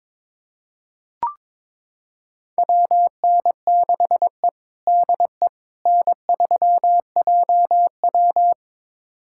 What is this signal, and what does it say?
Morse code, heard as a keyed tone, about 22 wpm, 700 Hz. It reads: WN6E DE N3JW
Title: USA Callsign DE Callsign - 4 Characters - 22wpm